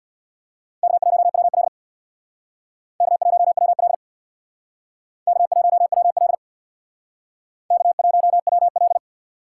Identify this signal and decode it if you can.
Morse code, heard as a keyed tone, about 66 wpm, 700 Hz. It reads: K1WL K1WL K1WL K1WL